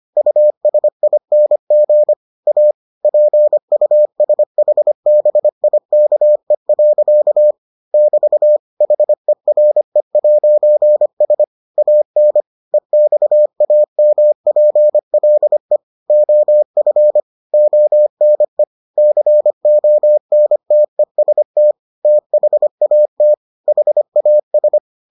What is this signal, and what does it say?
Morse code, heard as a keyed tone, about 25 wpm, 600 Hz. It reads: USING A PUSHBIKE. <BT> HERE'S AN EXAMPLE OF ONE CONTEST THAT HAS